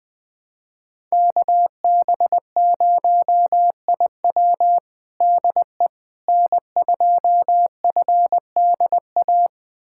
Morse code, heard as a keyed tone, about 20 wpm, 700 Hz. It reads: KB0IW DE N2FDA
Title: USA Callsign DE Callsign 20wpm